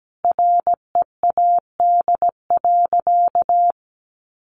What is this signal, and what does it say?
Morse code, heard as a keyed tone, about 17 wpm, 700 Hz. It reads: READ.
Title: Sentences from Top 1100 Words 17wpm